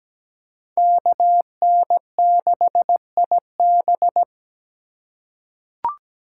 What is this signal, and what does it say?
Morse code, heard as a keyed tone, about 17 wpm, 700 Hz. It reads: KN6IB
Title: USA Call Signs - 5 Characters - 2x Character Spacing - 17wpm